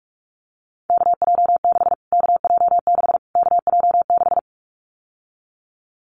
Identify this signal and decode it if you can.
Morse code, heard as a keyed tone, about 44 wpm, 700 Hz. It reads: KJ6 KJ6 KJ6
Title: Callsign Prefixes for Top 25 DXCC - Repeated 3x - 45wpm